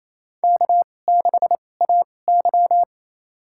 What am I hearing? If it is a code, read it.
Morse code, heard as a keyed tone, about 28 wpm, spaced as for 20 wpm, 700 Hz. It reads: K6AY
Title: USA Call Signs - 4 Characters - 2x Character Spacing - 28wpm